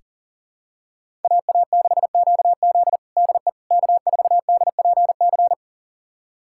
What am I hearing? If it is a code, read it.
Morse code, heard as a keyed tone, about 40 wpm, 700 Hz. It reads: AA6QZ DE K4DPC
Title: USA Callsign DE Callsign - 5 Characters - 40wpm